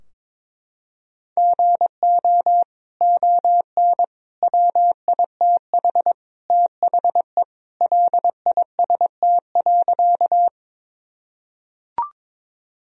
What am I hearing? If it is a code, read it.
Morse code, heard as a keyed tone, about 22 wpm, 700 Hz. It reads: GO ON WITH THE LIST.
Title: Sentences from Top 1000 Words - Review Part 2 - 22wpm